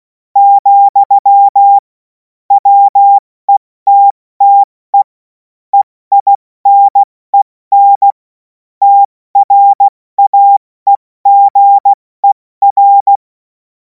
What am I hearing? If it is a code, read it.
Morse code, heard as a keyed tone, about 16 wpm, spaced as for 14 wpm, 800 Hz. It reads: , WETTE EINEN TRAEGER
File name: Morsekurs / Witze / Witz 07. April 2026 70 F80